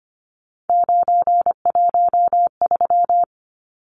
Morse code, heard as a keyed tone, about 25 wpm, 700 Hz. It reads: 913
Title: Numbers - 3-Digit 25wpm